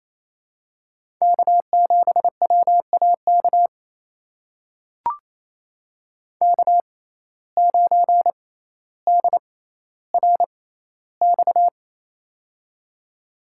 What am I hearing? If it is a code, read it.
Morse code, heard as a keyed tone, about 28 wpm, spaced as for 14 wpm, 700 Hz. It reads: K7WAK K9DRX
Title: USA Call Signs - 5 Characters - 6x Character Spacing - 28wpm